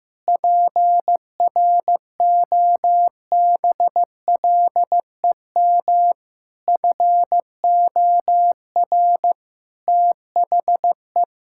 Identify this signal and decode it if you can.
Morse code, heard as a keyed tone, about 15 wpm, 700 Hz. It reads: PROBLEM FOR THE
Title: Sentences from Top 1000 Words - Review Part 5 - 15wpm